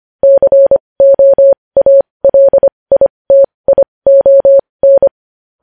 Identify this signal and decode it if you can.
Morse code, heard as a keyed tone, about 25 wpm, spaced as for 20 wpm, 550 Hz. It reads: COALITION
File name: podcast / News headlines-1x-25x20